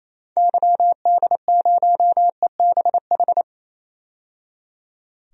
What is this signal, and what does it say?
Morse code, heard as a keyed tone, about 28 wpm, 700 Hz. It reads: YD0EBH
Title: DXCC-100 Call Signs - 6 Characters - 28wpm